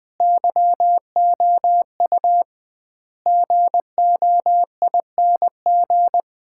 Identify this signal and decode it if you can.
Morse code, heard as a keyed tone, about 20 wpm, 700 Hz. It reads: YOU GOING
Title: 2x Word-Spacing - Sets of 5 Words - 20wpm